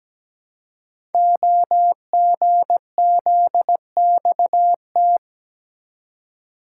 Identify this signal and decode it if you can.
Morse code, heard as a keyed tone, about 17 wpm, 700 Hz. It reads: OGZXT
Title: Code Groups - 5 Letters - 17wpm